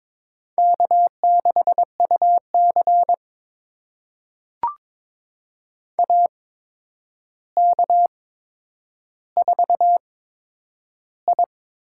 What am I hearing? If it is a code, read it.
Morse code, heard as a keyed tone, about 22 wpm, spaced as for 9 wpm, 700 Hz. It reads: K6UC AK4I
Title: USA Call Signs - 4 Characters - 8x Character Spacing - 22wpm